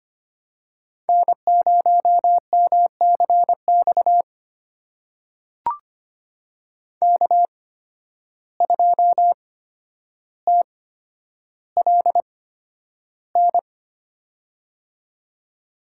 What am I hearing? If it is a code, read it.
Morse code, heard as a keyed tone, about 25 wpm, spaced as for 11 wpm, 700 Hz. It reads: N0MCX K2TLN
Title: USA Call Signs - 5 Characters - 8x Character Spacing - 25wpm